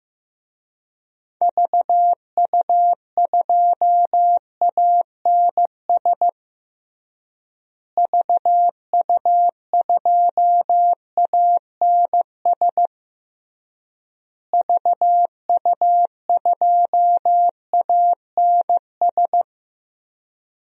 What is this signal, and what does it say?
Morse code, heard as a keyed tone, about 15 wpm, 700 Hz. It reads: VU2ANS VU2ANS VU2ANS
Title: DXCC-100 Call Signs - 6 Characters - Repeated 3x - 15wpm